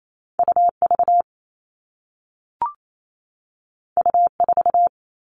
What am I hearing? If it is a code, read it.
Morse code, heard as a keyed tone, about 28 wpm, 700 Hz. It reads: UV U4
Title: All Cut Numbers - 2-Digit 28wpm